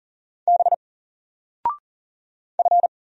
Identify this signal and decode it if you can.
Morse code, heard as a keyed tone, about 40 wpm, 700 Hz. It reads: B F
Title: Single Character (Letter:Number:Punctuation:Prosign) Alt 40wpm